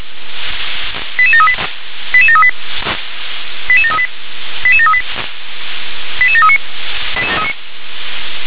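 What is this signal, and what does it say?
Five-tone selective call (ZVEI), heard as quick runs of five tones, tones of about 70 ms. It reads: 88738 88738 88738 88738 88739
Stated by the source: Received by SDR; ZVEI five-tone code 88737.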